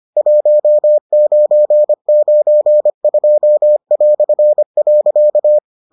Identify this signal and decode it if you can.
Morse code, heard as a keyed tone, about 25 wpm, 600 Hz. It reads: 1992".